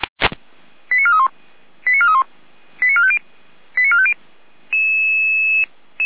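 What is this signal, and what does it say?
Five-tone selective call (ZVEI), heard as quick runs of five tones, tones of about 80 ms. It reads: 89431 89431 89450 89450
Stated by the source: ZVEI five-tone selective call received by SDR.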